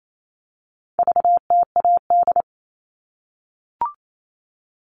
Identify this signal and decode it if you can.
Morse code, heard as a keyed tone, about 28 wpm, 700 Hz. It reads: VTAD
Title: All Cut Numbers - 4-Digit 28wpm